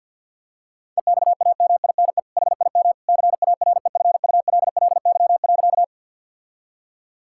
Morse code, heard as a keyed tone, about 50 wpm, 700 Hz. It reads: EXAMINE HIM CAREFULLY.
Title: Sentences from Top 1600 Words 50wpm